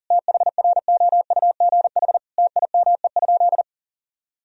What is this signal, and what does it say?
Morse code, heard as a keyed tone, about 40 wpm, 700 Hz. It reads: THROUGH TIME?
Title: Sentences from Top 1000 Words - Review Part 2 - 40wpm